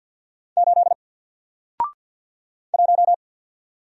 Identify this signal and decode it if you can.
Morse code, heard as a keyed tone, about 48 wpm, 700 Hz. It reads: 8 1
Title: Single Letter-Number Alt 50wpm